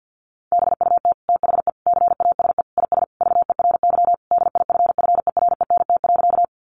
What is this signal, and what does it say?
Morse code, heard as a keyed tone, about 50 wpm, 700 Hz. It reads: BUT THE CASE IS VERY DIFFERENT.
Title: Sentences from Top 700 Words 50wpm